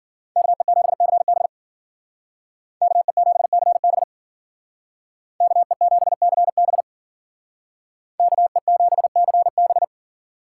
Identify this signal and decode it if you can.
Morse code, heard as a keyed tone, about 52 wpm, 700 Hz. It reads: KE7CB KE7CB KE7CB KE7CB